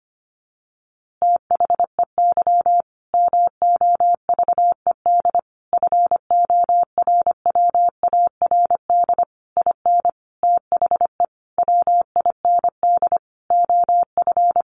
Morse code, heard as a keyed tone, about 25 wpm, 700 Hz. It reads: THEY MOVED FORWARD IN THE WIND OF
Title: A Sound Of Thunder 25wpm